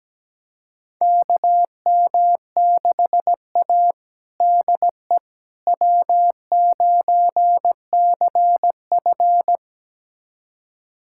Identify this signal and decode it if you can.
Morse code, heard as a keyed tone, about 17 wpm, 700 Hz. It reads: KM6A DE W9CF
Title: USA Callsign DE Callsign - 4 Characters - 17wpm